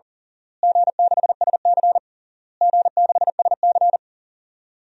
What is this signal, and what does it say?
Morse code, heard as a keyed tone, about 40 wpm, 700 Hz. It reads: G6SC G6SC
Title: DXCC-100 Call Signs - 4 Characters - Repeated 3x - 40wpm